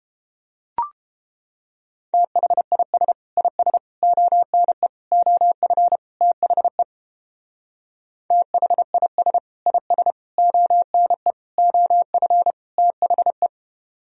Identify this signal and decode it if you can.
Morse code, heard as a keyed tone, about 29 wpm, 700 Hz. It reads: THIS IS ONE OF THE THIS IS ONE OF THE